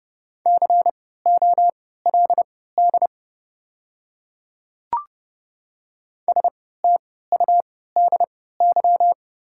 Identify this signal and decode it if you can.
Morse code, heard as a keyed tone, about 30 wpm, spaced as for 17 wpm, 700 Hz. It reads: COLD STUDY
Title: 3x Character Spacing - Top 500 Words - 30wpm